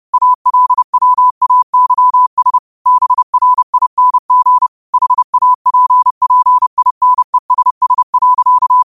Morse code, heard as a keyed tone, about 30 wpm, 1000 Hz. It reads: ALWAYS BRING HAPPINESS.
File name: qotdfiles / 30wpm